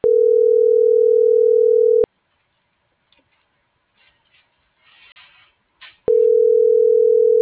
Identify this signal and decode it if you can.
Ringback tone (North America), 440 + 480 Hz.